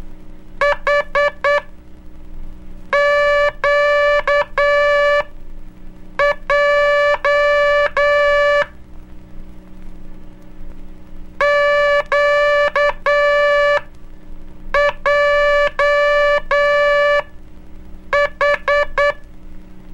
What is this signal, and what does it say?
Morse code, heard as a keyed tone, about 9 wpm, spaced as for 5 wpm, 580 Hz. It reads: HQJ QJH
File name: cursos / Curso CW mp3 / CW Aula - 04